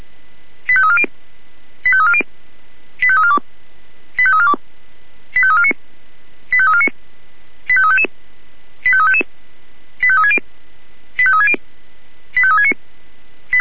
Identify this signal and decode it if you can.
Five-tone selective call (ZVEI), heard as quick runs of five tones, tones of about 70 ms. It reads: 85340 85340 85342 85342 85349 85349 85355 85355 85360 85360 85369
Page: SDR/ZVEI